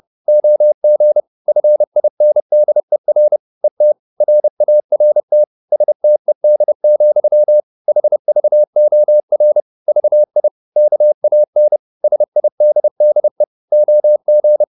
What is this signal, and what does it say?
Morse code, heard as a keyed tone, about 30 wpm, 600 Hz. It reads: OG FINDER ET RART STED, HVOR VI KAN SIDDE OG
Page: Ferie 30 wpm